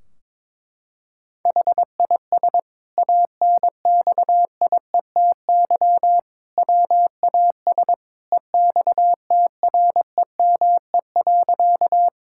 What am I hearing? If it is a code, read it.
Morse code, heard as a keyed tone, about 22 wpm, 700 Hz. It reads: HIS ANXIETY WAS EXTREME.
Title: Sentences from Top 1600 Words 22wpm